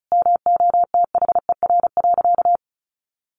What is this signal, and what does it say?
Morse code, heard as a keyed tone, about 34 wpm, 700 Hz. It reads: MOTHER.
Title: Sentences from Top 700 Words 35wpm